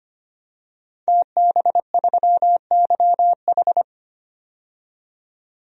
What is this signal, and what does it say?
Morse code, heard as a keyed tone, about 25 wpm, 700 Hz. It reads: TB3YH
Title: DXCC-100 Call Signs - 5 Characters - 25wpm